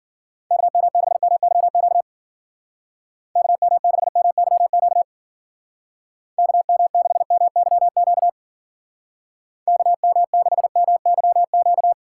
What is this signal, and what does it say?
Morse code, heard as a keyed tone, about 51 wpm, 700 Hz. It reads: KM6MYQ KM6MYQ KM6MYQ KM6MYQ